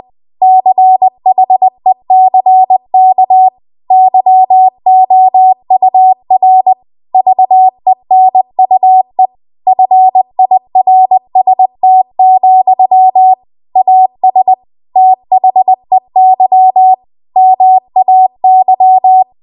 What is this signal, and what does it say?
Morse code, heard as a keyed tone, about 20 wpm, 750 Hz. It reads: CHECK YOUR VENUE FIRST, AS THEY MAY